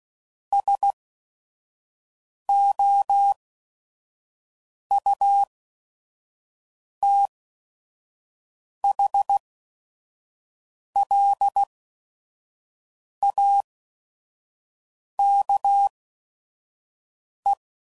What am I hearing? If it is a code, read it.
Morse code, heard as a keyed tone, about 16 wpm, spaced as for 5 wpm, 800 Hz. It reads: SOUTHLAKE
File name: CW Code Groups @5WPM / fromSM's / Cities